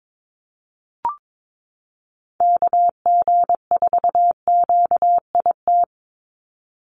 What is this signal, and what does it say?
Morse code, heard as a keyed tone, about 22 wpm, 700 Hz. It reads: KG4QIT
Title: USA Call Signs 22wpm